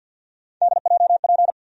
Morse code, heard as a keyed tone, about 54 wpm, 700 Hz. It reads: DJP
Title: USA Call Sign Postfixes - 3 Letters 50wpm